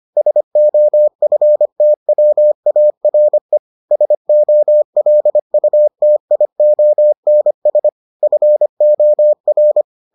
Morse code, heard as a keyed tone, about 25 wpm, 600 Hz. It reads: SOFTWARE SOLUTIONS FOR